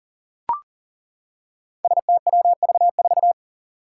Morse code, heard as a keyed tone, about 40 wpm, 700 Hz. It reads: STWV4
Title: Five Challenging Letter-Number Combinations 40wpm